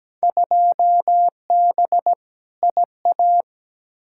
Morse code, heard as a keyed tone, about 17 wpm, 700 Hz. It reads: 2B IA
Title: Field Day Exchanges Encore - 17wpm